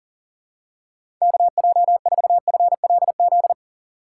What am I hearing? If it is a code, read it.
Morse code, heard as a keyed tone, about 40 wpm, 700 Hz. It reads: KJ4FLZ
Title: USA Call Signs - 6x Character Spacing - 40wpm